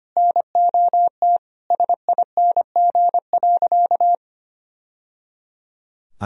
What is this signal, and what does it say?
Morse code, heard as a keyed tone, about 25 wpm, 700 Hz. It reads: NOT SING.
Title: Sentences from Top 1000 Words - Review Part 2 - 25wpm